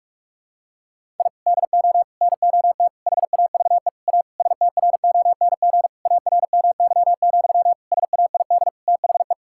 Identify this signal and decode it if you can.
Morse code, heard as a keyed tone, about 46 wpm, 700 Hz. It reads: I DO NOT HAVE A STRONG ARMY, SAID THE
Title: Sentences from Top 1000 Words - Review Part 5 - 45wpm